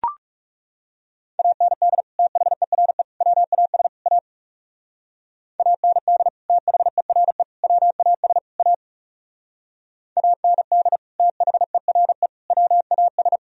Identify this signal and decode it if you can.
Morse code, heard as a keyed tone, about 40 wpm, 700 Hz. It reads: AND THERE WAS A AND THERE WAS A AND THERE WAS